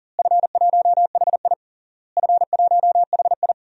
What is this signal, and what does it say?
Morse code, heard as a keyed tone, about 40 wpm, 700 Hz. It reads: F1HI F1HI
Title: DXCC-100 Call Signs - 4 Characters - Repeated 3x - 40wpm